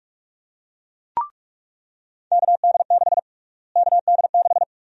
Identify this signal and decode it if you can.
Morse code, heard as a keyed tone, about 46 wpm, 700 Hz. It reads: KD6 KD6
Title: USA Call Sign Prefixes - Repeated 2x - 45wpm